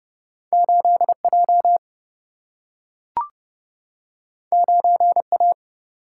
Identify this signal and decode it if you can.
Morse code, heard as a keyed tone, about 30 wpm, 700 Hz. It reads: 8 J 9 A
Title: Single-Letter Single-Number 30wpm